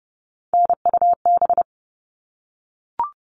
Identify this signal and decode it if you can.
Morse code, heard as a keyed tone, about 30 wpm, 700 Hz. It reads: NUB
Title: All Cut Numbers - 3-Digit 30wpm